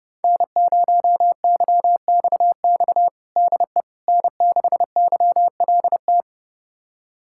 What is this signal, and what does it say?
Morse code, heard as a keyed tone, about 30 wpm, 700 Hz. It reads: N0YXX DE N6YLT
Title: USA Callsign DE Callsign - 5 Characters - 30wpm